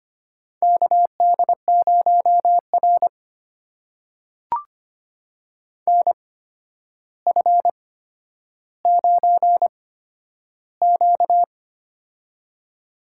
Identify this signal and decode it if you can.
Morse code, heard as a keyed tone, about 25 wpm, spaced as for 11 wpm, 700 Hz. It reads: KD0R NF9Q
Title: USA Call Signs - 4 Characters - 8x Character Spacing - 25wpm